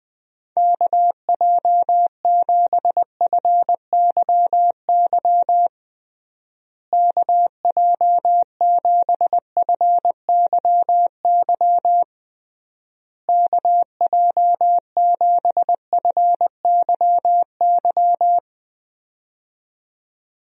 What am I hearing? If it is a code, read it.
Morse code, heard as a keyed tone, about 20 wpm, 700 Hz. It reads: KJ7FYY KJ7FYY KJ7FYY
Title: USA Call Signs - 6 Characters - Repeated 3x - 20wpm